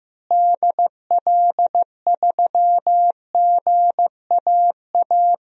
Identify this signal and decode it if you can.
Morse code, heard as a keyed tone, about 15 wpm, 700 Hz. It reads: DL3GAA
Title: DXCC-100 Call Signs - 6 Characters - Repeated 3x - 15wpm